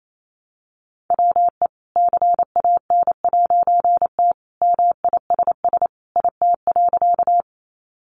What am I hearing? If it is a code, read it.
Morse code, heard as a keyed tone, about 28 wpm, 700 Hz. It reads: WE CAN'T MISS IT.
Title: A Sound Of Thunder 28wpm